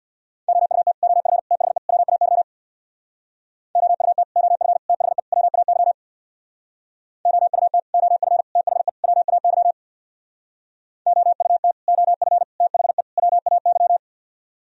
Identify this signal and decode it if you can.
Morse code, heard as a keyed tone, about 64 wpm, 700 Hz. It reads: OUT OF THE WAY OUT OF THE WAY OUT OF THE WAY OUT OF THE WAY